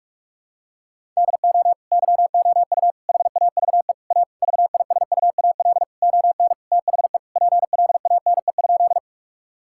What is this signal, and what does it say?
Morse code, heard as a keyed tone, about 46 wpm, 700 Hz. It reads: DO YOU HAVE A VISUAL ON THE PLANE?
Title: Sentences from Top 1000 Words - Review Part 6 - 45wpm